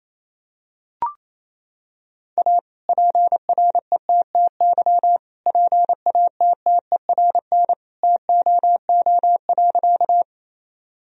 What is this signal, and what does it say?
Morse code, heard as a keyed tone, about 28 wpm, 700 Hz. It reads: A PRETTY PATTERN TOO.
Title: Sentences from Top 1000 Words - Review Part 2 - 28wpm